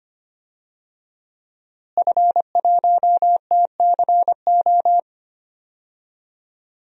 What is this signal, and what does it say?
Morse code, heard as a keyed tone, about 25 wpm, 700 Hz. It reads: F1TCO
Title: DXCC-100 Call Signs - 5 Characters - Repeated 3x - 25wpm